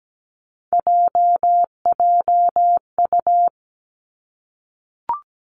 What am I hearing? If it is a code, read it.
Morse code, heard as a keyed tone, about 17 wpm, 700 Hz. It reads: JJU
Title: Three Challenging Letter-Number Combinations 17wpm